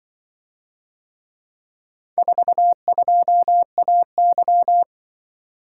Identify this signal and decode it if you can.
Morse code, heard as a keyed tone, about 24 wpm, 700 Hz. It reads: 42AY
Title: Speed-Racing - Four Letter-Number Combinations - 20wpm